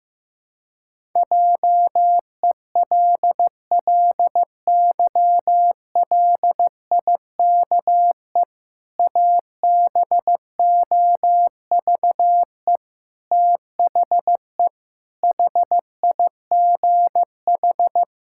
Morse code, heard as a keyed tone, about 15 wpm, 700 Hz. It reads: JELLYLIKE ABOVE THE HIGH
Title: Pangram Challenge v2 - 15wpm